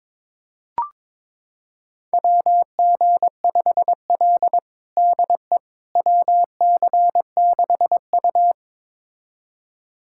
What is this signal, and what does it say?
Morse code, heard as a keyed tone, about 21 wpm, 700 Hz. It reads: WG5L DE WC6U
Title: USA Callsign DE Callsign - 4 Characters - 22wpm